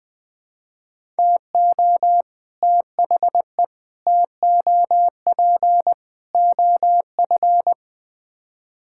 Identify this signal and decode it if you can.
Morse code, heard as a keyed tone, about 20 wpm, 700 Hz. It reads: TO THE TOP OF
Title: Build A Phrase - 4-Word Phrases - 20wpm